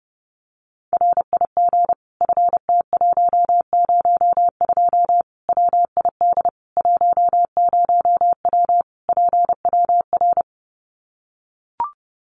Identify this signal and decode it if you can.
Morse code, heard as a keyed tone, about 30 wpm, 700 Hz. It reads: RIG FT102 WID 10W PWR
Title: Common QSO Phrases 30wpm